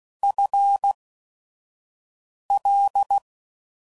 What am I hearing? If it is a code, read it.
Morse code, heard as a keyed tone, about 16 wpm, spaced as for 5 wpm, 800 Hz. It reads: FL
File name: Prefixes / FLs & THs / FL Prefixes 1